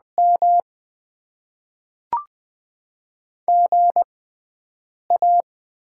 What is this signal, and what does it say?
Morse code, heard as a keyed tone, about 20 wpm, spaced as for 7 wpm, 690 Hz. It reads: M GA
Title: Top 20 QSO Elements - 6x Character Spacing - 20wpm